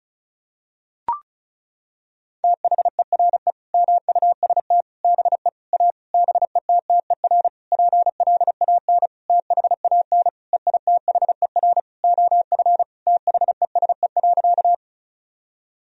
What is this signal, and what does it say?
Morse code, heard as a keyed tone, about 36 wpm, 700 Hz. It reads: THERE MUST BE A BETTER PLAN THAN EITHER OF THESE.
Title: Sentences from Top 1000 Words - Review Part 4 - 35wpm